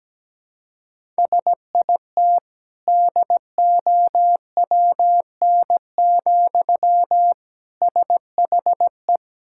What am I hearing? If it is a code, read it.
Morse code, heard as a keyed tone, about 17 wpm, 700 Hz. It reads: SIT DOWN, SHE
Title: Sentences from Top 1000 Words - Review Part 6 - 17wpm